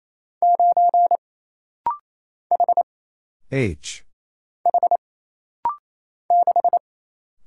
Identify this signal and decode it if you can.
Morse code, heard as a keyed tone, about 28 wpm, 700 Hz. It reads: 9 H H 6
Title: Single Letter-Number - Rapid-Fire 28wpm